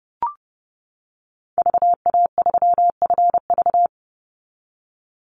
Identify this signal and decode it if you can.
Morse code, heard as a keyed tone, about 30 wpm, 700 Hz. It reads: VA3FV
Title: Call Signs - 30wpm